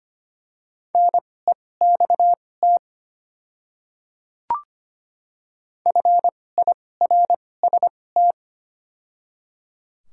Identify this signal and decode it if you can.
Morse code, heard as a keyed tone, about 25 wpm, spaced as for 18 wpm, 700 Hz. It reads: NEXT FIRST
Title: 2x Character Spacing - Top 300 Words - 25wpm